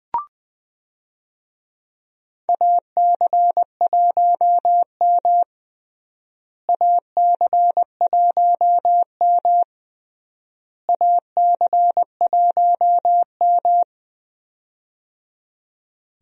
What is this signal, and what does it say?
Morse code, heard as a keyed tone, about 20 wpm, 700 Hz. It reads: AC1M AC1M AC1M
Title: USA Call Signs - 4 Characters - Repeated 3x - 20wpm